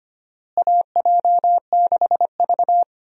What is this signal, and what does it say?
Morse code, heard as a keyed tone, about 25 wpm, 700 Hz. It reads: AJ6V
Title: USA Call Signs - 4 Characters - Repeated 3x - 25wpm